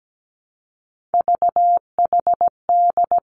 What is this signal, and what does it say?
Morse code, heard as a keyed tone, about 17 wpm, 700 Hz. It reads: VHD
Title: Three Challenging Letter-Number Combinations 17wpm